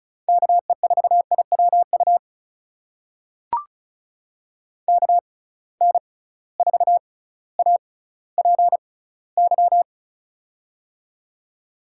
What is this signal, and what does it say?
Morse code, heard as a keyed tone, about 36 wpm, spaced as for 18 wpm, 700 Hz. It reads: KE4IWU KN4APY
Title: USA Call Signs - 6 Characters - 6x Character Spacing - 35wpm